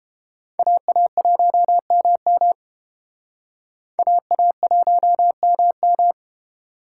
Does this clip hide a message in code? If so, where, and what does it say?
Yes: Morse code from 0.6 to 6.1 s, heard as a keyed tone, about 29 wpm, 700 Hz. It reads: AA1MM AA1MM